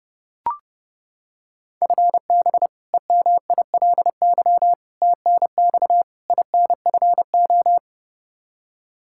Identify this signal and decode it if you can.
Morse code, heard as a keyed tone, about 30 wpm, 700 Hz. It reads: FB EMILY TNX INFO
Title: Practice QSO - Exchange 3 - More Info - 30wpm